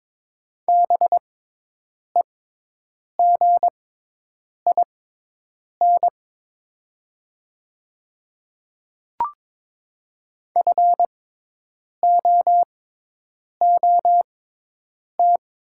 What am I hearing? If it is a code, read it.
Morse code, heard as a keyed tone, about 21 wpm, spaced as for 8 wpm, 700 Hz. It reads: BEGIN FOOT
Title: 6x Character Spacing - Top 500 Words - 22wpm